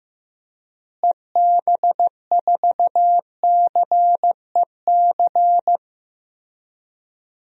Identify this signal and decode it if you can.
Morse code, heard as a keyed tone, about 15 wpm, 700 Hz. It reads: EB4CEC
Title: DXCC-100 Call Signs - 6 Characters - Repeated 3x - 15wpm